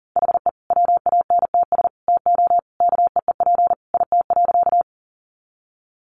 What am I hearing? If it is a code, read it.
Morse code, heard as a keyed tone, about 40 wpm, 700 Hz. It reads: HE WANTS TO KEEP IT.
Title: Additional Easy Phrases 40wpm